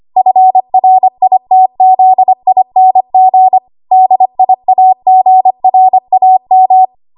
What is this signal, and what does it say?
Morse code, heard as a keyed tone, about 25 wpm, 750 Hz. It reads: FRITZING DIAGRAM
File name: Archive / 25 WPM / 25WPM